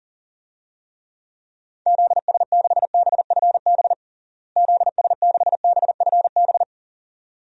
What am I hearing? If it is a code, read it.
Morse code, heard as a keyed tone, about 40 wpm, 700 Hz. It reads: ZS6BFB ZS6BFB
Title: DXCC-100 Call Signs - 6 Characters - Repeated 2x - 40wpm